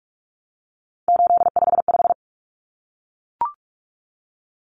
Morse code, heard as a keyed tone, about 44 wpm, 700 Hz. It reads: 855
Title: Numbers - 3-Digit 45wpm